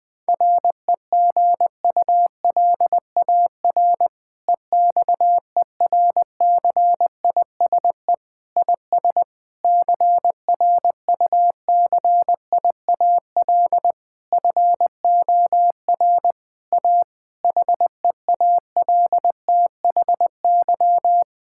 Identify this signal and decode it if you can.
Morse code, heard as a keyed tone, about 20 wpm, 700 Hz. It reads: REGULAR EXERCISE IS CRUCIAL FOR A HEALTHY
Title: Difficult Sentences with Hints - 20wpm